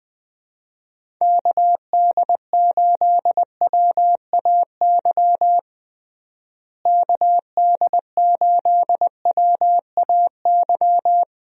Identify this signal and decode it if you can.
Morse code, heard as a keyed tone, about 20 wpm, 700 Hz. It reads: KD8WAY KD8WAY
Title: USA Call Signs - 6 Characters - Repeated 2x - 20wpm